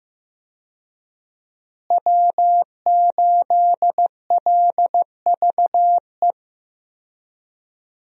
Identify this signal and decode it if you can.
Morse code, heard as a keyed tone, about 15 wpm, 700 Hz. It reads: W8LVE
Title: USA Call Signs - 5 Characters - Repeated 3x - 15wpm